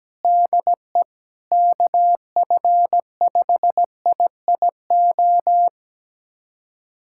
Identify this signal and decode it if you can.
Morse code, heard as a keyed tone, about 17 wpm, 700 Hz. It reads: DE KF5IIO
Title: USA Callsign DE Callsign - 6 Characters - 17wpm